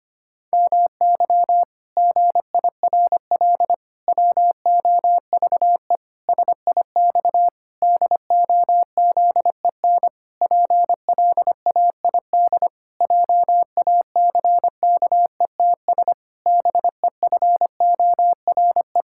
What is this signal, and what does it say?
Morse code, heard as a keyed tone, about 25 wpm, 700 Hz. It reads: MY GIRL WOVE SIX DOZEN PLAID JACKETS BEFORE
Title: Pangram Challenge v2 - 25wpm